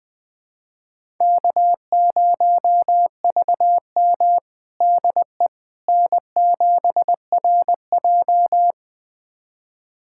Morse code, heard as a keyed tone, about 20 wpm, 700 Hz. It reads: K0VM DE N7RJ
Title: USA Callsign DE Callsign - 4 Characters - 20wpm